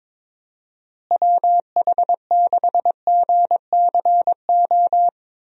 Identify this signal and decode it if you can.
Morse code, heard as a keyed tone, about 22 wpm, 700 Hz. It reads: WH6GCO
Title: USA Call Signs - 6 Characters - Repeated 3x - 22wpm